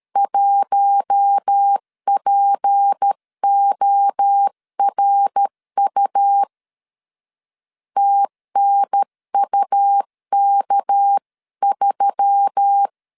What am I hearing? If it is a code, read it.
Morse code, heard as a keyed tone, about 13 wpm, 800 Hz. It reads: 1PORU TNUK3
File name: CW / DET / 1 6 1500C S C 12wpm 3533